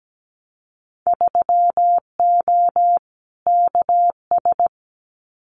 Morse code, heard as a keyed tone, about 17 wpm, 700 Hz. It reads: 3O KS
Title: Winter Field Day Exchanges 17wpm